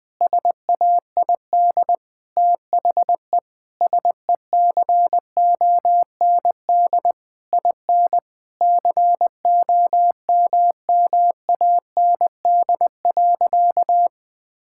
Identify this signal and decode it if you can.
Morse code, heard as a keyed tone, about 20 wpm, 700 Hz. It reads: SAID THE SECOND IN COMMAND.
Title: Sentences from Top 1600 Words 20wpm